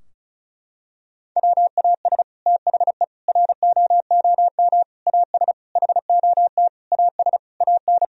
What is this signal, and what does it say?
Morse code, heard as a keyed tone, about 34 wpm, 700 Hz. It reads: WAS THE ROOM AS HOT AS AN
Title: Sentences from Top 1000 Words - Review Part 4 - 35wpm